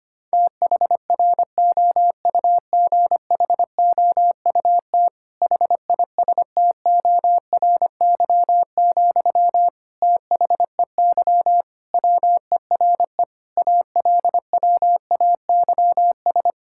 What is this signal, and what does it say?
Morse code, heard as a keyed tone, about 25 wpm, 700 Hz. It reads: THROUGHOUT HISTORY, THEY WERE ALWAYS
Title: Sentences from Top 1000 Words - Review Part 5 - 25wpm